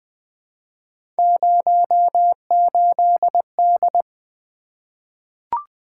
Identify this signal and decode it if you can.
Morse code, heard as a keyed tone, about 20 wpm, 700 Hz. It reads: 08D
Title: Three Letter-Number Combinations - 20wpm